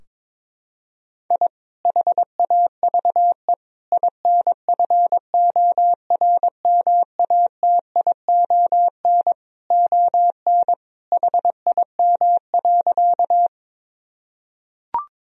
Morse code, heard as a keyed tone, about 22 wpm, 700 Hz. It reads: I HAVE INFORMATION ON HIM.
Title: Sentences from Top 1000 Words - Review Part 3 - 22wpm